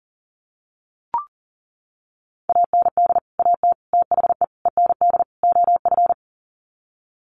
Morse code, heard as a keyed tone, about 40 wpm, 700 Hz. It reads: AND AT THE END OF
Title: Sets of 5 Words 40wpm